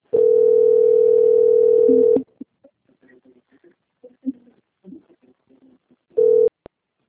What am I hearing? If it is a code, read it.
Ringback tone (North America), 440 + 480 Hz.